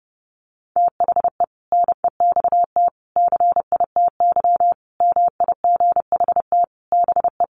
Morse code, heard as a keyed tone, about 30 wpm, 700 Hz. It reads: THE NEXT CITY MIGHT BE
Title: Sentences from Top 1400 Words 30wpm